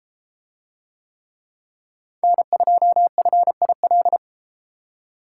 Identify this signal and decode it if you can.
Morse code, heard as a keyed tone, about 30 wpm, 700 Hz. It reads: N2FIL